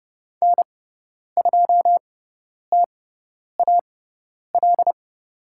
Morse code, heard as a keyed tone, about 30 wpm, spaced as for 10 wpm, 700 Hz. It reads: N2TAL
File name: Lesson 16g - Letter c - Call Signs 30wpm